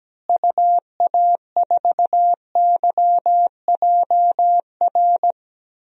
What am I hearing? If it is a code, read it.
Morse code, heard as a keyed tone, about 17 wpm, 700 Hz. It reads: UA4YJR
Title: DXCC-100 Call Signs - 6 Characters - Repeated 3x - 17wpm